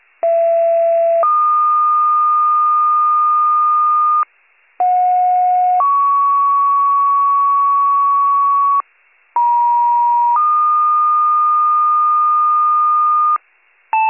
Two-tone page, heard as a one-second tone, then a three-second tone, about 670 then 1220 Hz, about 730 then 1120 Hz, about 950 then 1220 Hz. It is heard from 0.2 to 13.4 s.